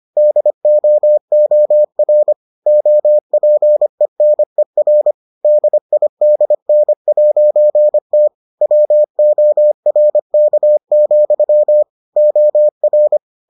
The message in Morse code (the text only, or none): DOOR OPENER DIDN'T WORK, OR